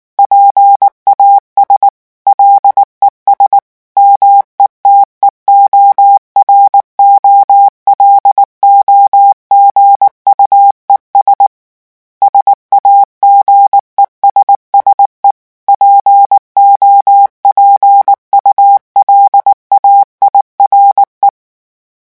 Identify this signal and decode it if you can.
Morse code, heard as a keyed tone, about 19 wpm, 800 Hz. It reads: PAS LES METEOROLOGUES SAGESSE POPULAIRE
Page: REVISION 8 - 8 mots minute